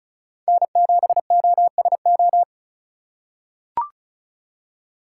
Morse code, heard as a keyed tone, about 36 wpm, 700 Hz. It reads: N7OSO
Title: USA Call Signs - 5 Characters - 4x Character Spacing - 35wpm